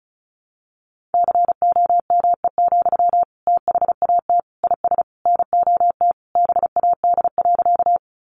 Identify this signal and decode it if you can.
Morse code, heard as a keyed tone, about 34 wpm, 700 Hz. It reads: COME, THAT IS NOT BAD.
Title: Sentences from Top 700 Words 35wpm